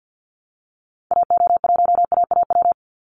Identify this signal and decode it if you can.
Morse code, heard as a keyed tone, about 48 wpm, 700 Hz. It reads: AO1AAW
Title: Call Signs Repeated 3x - 50wpm